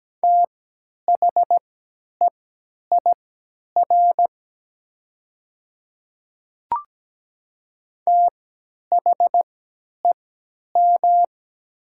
Morse code, heard as a keyed tone, about 17 wpm, spaced as for 10 wpm, 700 Hz. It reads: THEIR THEM
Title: 3x Character Spacing - Top 200 Words - 17wpm